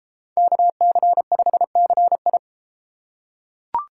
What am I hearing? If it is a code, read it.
Morse code, heard as a keyed tone, about 31 wpm, 700 Hz. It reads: KC5CI